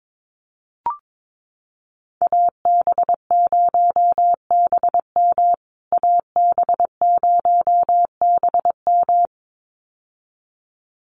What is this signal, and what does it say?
Morse code, heard as a keyed tone, about 22 wpm, 700 Hz. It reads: AB0BM AB0BM
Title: USA Call Signs Repeated 2x 22wpm